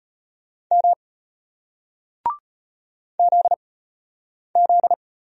Morse code, heard as a keyed tone, about 35 wpm, spaced as for 8 wpm, 700 Hz. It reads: M ZZ